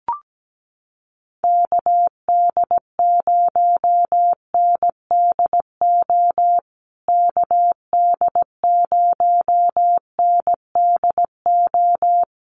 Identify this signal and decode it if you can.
Morse code, heard as a keyed tone, about 17 wpm, 700 Hz. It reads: KD0NDO KD0NDO
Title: USA Call Signs Repeated 3x 17wpm